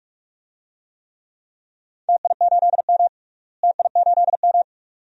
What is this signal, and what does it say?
Morse code, heard as a keyed tone, about 46 wpm, 700 Hz. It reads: TI8M TI8M
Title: DXCC-100 Call Signs - 4 Characters - Repeated 2x - 45wpm